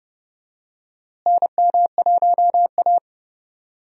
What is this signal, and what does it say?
Morse code, heard as a keyed tone, about 30 wpm, 700 Hz. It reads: NM1A
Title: USA Call Signs - 4 Characters - Repeated 2x - 30wpm